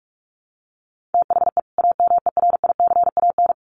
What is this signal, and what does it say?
Morse code, heard as a keyed tone, about 44 wpm, 700 Hz. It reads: THE AMERICAN
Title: Sets of 2 Words 45wpm